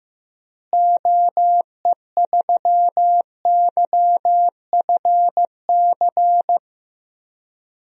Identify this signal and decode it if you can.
Morse code, heard as a keyed tone, about 15 wpm, 700 Hz. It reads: OE3YFC
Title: DXCC-100 Call Signs - 6 Characters - Repeated 2x - 15wpm